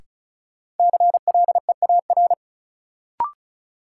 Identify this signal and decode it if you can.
Morse code, heard as a keyed tone, about 36 wpm, 700 Hz. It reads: CLEAR
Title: Sentences from Top 500 Words - New Word Drill - 35wpm